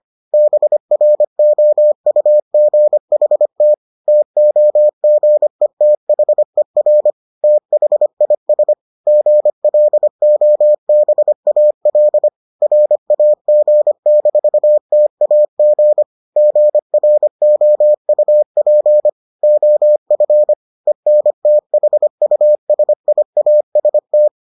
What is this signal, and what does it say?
Morse code, heard as a keyed tone, about 25 wpm, 600 Hz. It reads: BROUGHT TOGETHER THIS GLOBAL RAG-TAG GROUP OF ENTHUSIAST